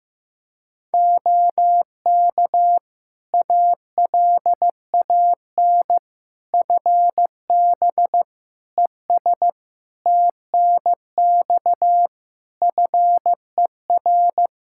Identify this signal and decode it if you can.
Morse code, heard as a keyed tone, about 15 wpm, 700 Hz. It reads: OK ALAN FB ES TNX FER
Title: Practice QSO - Exchange 2 - Info - 15wpm